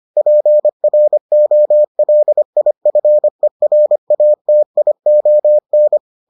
Morse code, heard as a keyed tone, about 25 wpm, 600 Hz. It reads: PROLIFERATION